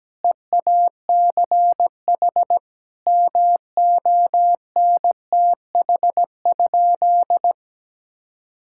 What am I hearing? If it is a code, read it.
Morse code, heard as a keyed tone, about 17 wpm, 700 Hz. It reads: EACH MONTH?
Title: Sentences from Top 1000 Words - Review Part 6 - 17wpm